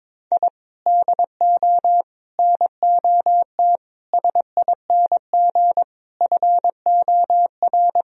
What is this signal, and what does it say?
Morse code, heard as a keyed tone, about 22 wpm, 700 Hz. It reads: I DO NOT SING FOR
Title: Sentences from Top 1000 Words - Review Part 2 - 22wpm